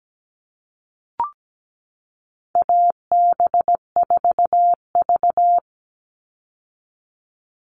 Morse code, heard as a keyed tone, about 17 wpm, 700 Hz. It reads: AB4V
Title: All Cut Numbers - 4-Digit 17wpm